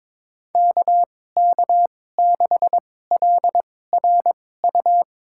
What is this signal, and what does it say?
Morse code, heard as a keyed tone, about 22 wpm, spaced as for 16 wpm, 700 Hz. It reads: KK6LRU
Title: USA Call Signs - 6 Characters - 2x Character Spacing - 22wpm